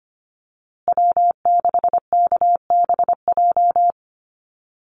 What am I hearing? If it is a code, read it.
Morse code, heard as a keyed tone, about 25 wpm, 700 Hz. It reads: W6KBJ
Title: USA Call Signs Repeated 3x 25wpm